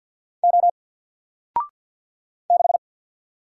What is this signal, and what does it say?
Morse code, heard as a keyed tone, about 48 wpm, 700 Hz. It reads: O 6
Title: Single Character (Letter:Number:Punctuation:Prosign) Alt 50wpm